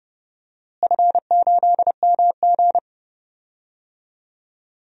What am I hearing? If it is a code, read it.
Morse code, heard as a keyed tone, about 30 wpm, 700 Hz. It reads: F8MG
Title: DXCC-100 Call Signs - 4 Characters - Repeated 3x - 30wpm